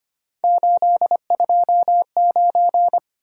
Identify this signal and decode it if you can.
Morse code, heard as a keyed tone, about 25 wpm, 700 Hz. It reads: 829
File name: Three Letter-Number Combinations - 25wpm